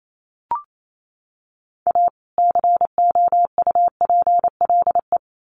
Morse code, heard as a keyed tone, about 28 wpm, 700 Hz. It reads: A COUPLE
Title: Sets of 2 Words 28wpm